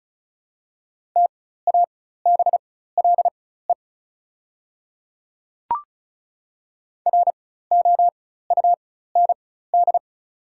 Morse code, heard as a keyed tone, about 34 wpm, spaced as for 16 wpm, 690 Hz. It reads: TABLE ROUND
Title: 4x Character Spacing - Top 500 Words - 35wpm